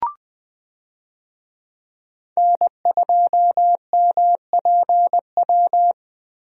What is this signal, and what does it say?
Morse code, heard as a keyed tone, about 20 wpm, 700 Hz. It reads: N2MPW
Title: USA Call Signs - 5 Characters - Repeated 3x - 20wpm